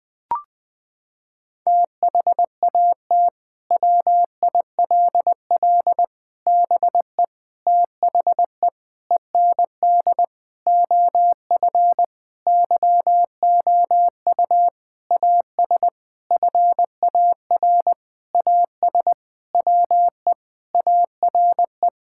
Morse code, heard as a keyed tone, about 20 wpm, 700 Hz. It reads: THAT WILL BE THE END OF YOU AS FAR AS WE ARE
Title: Sentences from Top 1000 Words - Review Part 6 - 20wpm